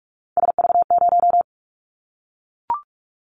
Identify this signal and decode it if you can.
Morse code, heard as a keyed tone, about 44 wpm, 700 Hz. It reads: SV0
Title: Callsign Prefixes for Top 50 DXCC - 45wpm